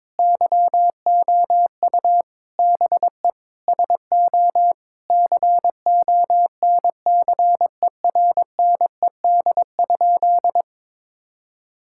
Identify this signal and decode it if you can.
Morse code, heard as a keyed tone, about 22 wpm, 700 Hz. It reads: YOU BE SO CONCERNED?
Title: Sentences from Top 1000 Words - Review Part 6 - 22wpm